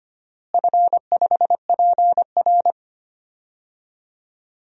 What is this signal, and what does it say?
Morse code, heard as a keyed tone, about 25 wpm, 700 Hz. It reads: F5PR
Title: DXCC-100 Call Signs - 4 Characters - Repeated 3x - 25wpm